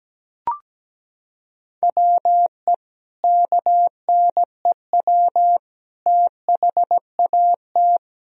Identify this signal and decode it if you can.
Morse code, heard as a keyed tone, about 17 wpm, 700 Hz. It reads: WE KNEW THAT
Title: Sentences from Top 400 Words and Abbreviations 17wpm